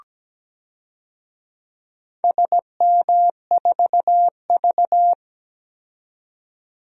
Morse code, heard as a keyed tone, about 17 wpm, 700 Hz. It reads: SM4V
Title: DXCC-100 Call Signs - 4 Characters - 17wpm